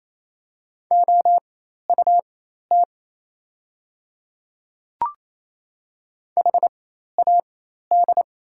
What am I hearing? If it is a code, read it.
Morse code, heard as a keyed tone, about 28 wpm, spaced as for 13 wpm, 700 Hz. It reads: OUT HAD
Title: 4x Character Spacing - Top 200 Words - 28wpm